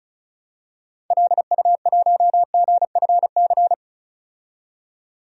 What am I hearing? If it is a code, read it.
Morse code, heard as a keyed tone, about 35 wpm, 700 Hz. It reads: LU1GFC
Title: DXCC-100 Call Signs - 6 Characters - 35wpm